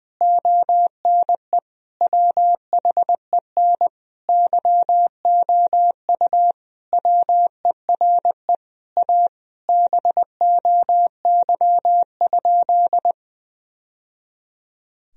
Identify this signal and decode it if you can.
Morse code, heard as a keyed tone, about 21 wpm, 700 Hz. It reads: ONE WHEN YOU WERE A BOY?
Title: Sentences from Top 1000 Words - Review Part 1 - 20wpm